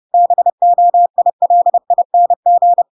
Morse code, heard as a keyed tone, about 30 wpm, 700 Hz. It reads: BOILING
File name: Pinocchio / Pinocchio ch3 30 30 ik1hge